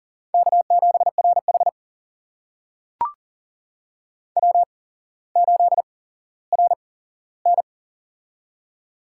Morse code, heard as a keyed tone, about 40 wpm, spaced as for 17 wpm, 700 Hz. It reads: K7RH W8RN
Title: USA Call Signs - 4 Characters - 8x Character Spacing - 40wpm